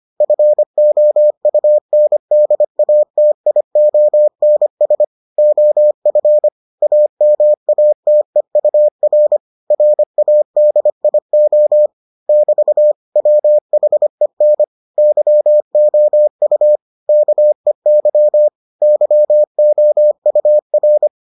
Morse code, heard as a keyed tone, about 25 wpm, 600 Hz. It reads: FOUNDATIONS OF AMATEUR RADIO <BT> WHEN YOU KEY YOUR